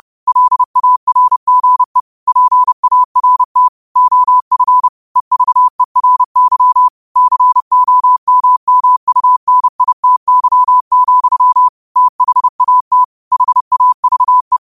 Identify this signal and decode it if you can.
Morse code, heard as a keyed tone, about 30 wpm, 1000 Hz. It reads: LARGE PART OF EVERY COMMUNITY, THAT HAVE
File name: qotdfiles / 30wpm